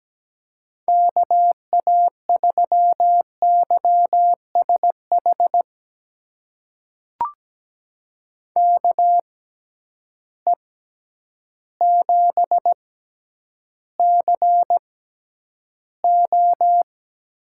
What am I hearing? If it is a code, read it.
Morse code, heard as a keyed tone, about 17 wpm, spaced as for 9 wpm, 700 Hz. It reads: KA3YSH KE7CO
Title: USA Call Signs - 6 Characters - 6x Character Spacing - 17wpm